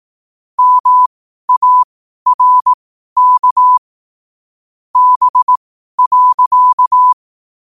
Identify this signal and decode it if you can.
Morse code, heard as a keyed tone, about 18 wpm, spaced as for 12 wpm, 1000 Hz. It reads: MARK B.
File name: qotdfiles / 12wpm